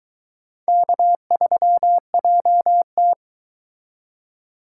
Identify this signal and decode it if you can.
Morse code, heard as a keyed tone, about 23 wpm, 700 Hz. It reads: K3JT